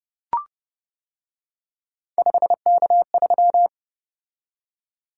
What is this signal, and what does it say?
Morse code, heard as a keyed tone, about 28 wpm, 700 Hz. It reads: 5K3